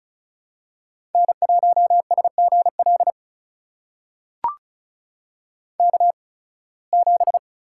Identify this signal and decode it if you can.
Morse code, heard as a keyed tone, about 36 wpm, spaced as for 23 wpm, 700 Hz. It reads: N1SGL K7
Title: USA Call Signs - 5 Characters - 8x Character Spacing - 35wpm